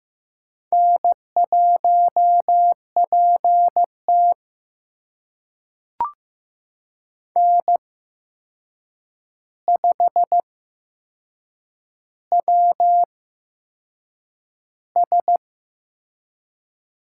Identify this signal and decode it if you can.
Morse code, heard as a keyed tone, about 15 wpm, spaced as for 6 wpm, 700 Hz. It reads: N1PT N5WS
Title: USA Call Signs - 4 Characters - 8x Character Spacing - 15wpm